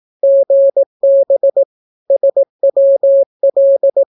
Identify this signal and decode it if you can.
Morse code, heard as a keyed tone, about 18 wpm, 550 Hz. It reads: GB SWL
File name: naukapodstaw / skroty / skroty